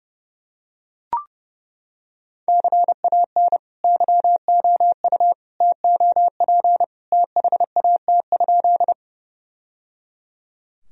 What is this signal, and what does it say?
Morse code, heard as a keyed tone, about 30 wpm, 700 Hz. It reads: CAN YOU TOP THAT?
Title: Sentences from Top 1000 Words - Review Part 2 - 30wpm